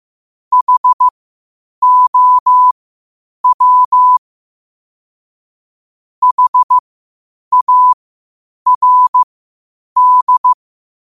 Morse code, heard as a keyed tone, about 15 wpm, spaced as for 8 wpm, 1000 Hz. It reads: HOW HARD
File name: qotdfiles / 8wpm